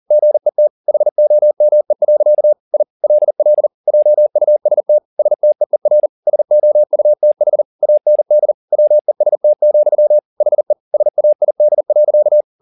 Morse code, heard as a keyed tone, about 40 wpm, 600 Hz. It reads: GET HOME. I LL JUST STEER SOUTH AND WEST, HE SAID.
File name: the old man and the sea 2 40wpm